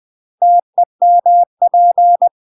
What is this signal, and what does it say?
Morse code, heard as a keyed tone, about 20 wpm, 700 Hz. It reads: TEMP
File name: UK MASTER QSOs TRAINING / 20WPM / UK MASTER QSOs 0059